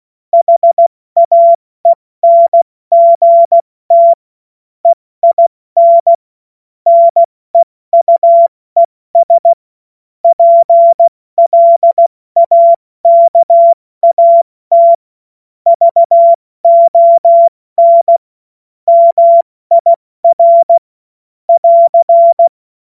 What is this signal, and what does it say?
Morse code, heard as a keyed tone, about 16 wpm, 680 Hz. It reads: HAENGT EIN NEUES PLAKAT VON MIR <AR>
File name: Morsekurs / Witze / Witz 27. April 2021, 70 F80